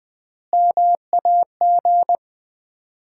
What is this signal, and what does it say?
Morse code, heard as a keyed tone, about 20 wpm, 700 Hz. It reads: MAG
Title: Common QSO Phrases - 4x Word-Spacing - 20wpm